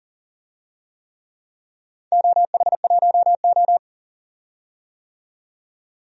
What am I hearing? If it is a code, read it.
Morse code, heard as a keyed tone, about 40 wpm, 700 Hz. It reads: OH1O
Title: DXCC-100 Call Signs - 4 Characters - 40wpm